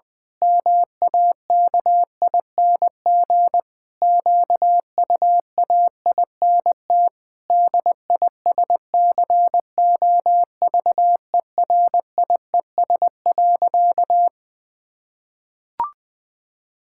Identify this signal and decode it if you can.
Morse code, heard as a keyed tone, about 20 wpm, 700 Hz. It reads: MAKING QUAINT DISCOVERIES.
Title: Pangram Challenge v2 - 20wpm